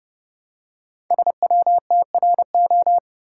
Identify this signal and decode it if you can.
Morse code, heard as a keyed tone, about 30 wpm, 700 Hz. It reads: SWTRO
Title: Code Groups - 5 Letters - 30wpm